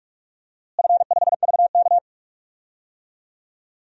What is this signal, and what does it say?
Morse code, heard as a keyed tone, about 46 wpm, 700 Hz. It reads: F5VK
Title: DXCC-100 Call Signs - 4 Characters - Repeated 2x - 45wpm